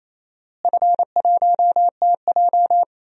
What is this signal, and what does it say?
Morse code, heard as a keyed tone, about 28 wpm, 700 Hz. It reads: F1TJ
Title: DXCC-100 Call Signs - 4 Characters - Repeated 3x - 28wpm